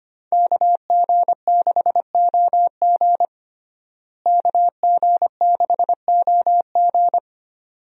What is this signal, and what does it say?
Morse code, heard as a keyed tone, about 25 wpm, 700 Hz. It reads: KG6OG KG6OG
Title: USA Call Signs - 5 Characters - Repeated 2x - 25wpm